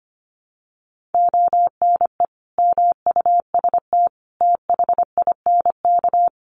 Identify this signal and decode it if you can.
Morse code, heard as a keyed tone, about 25 wpm, 700 Hz. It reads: ONE MUST THINK
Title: Sentences from Top 700 Words 25wpm